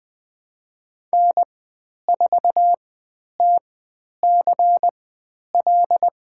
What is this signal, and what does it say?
Morse code, heard as a keyed tone, about 20 wpm, spaced as for 10 wpm, 700 Hz. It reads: N4TCL
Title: 095 Lesson 16g - Letter c - Call Signs 20wpm